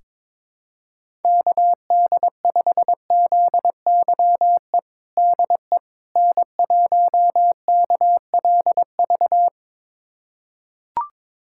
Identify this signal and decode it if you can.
Morse code, heard as a keyed tone, about 22 wpm, 700 Hz. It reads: KD5ZYE DE N1KLV
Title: USA Callsign DE Callsign 22wpm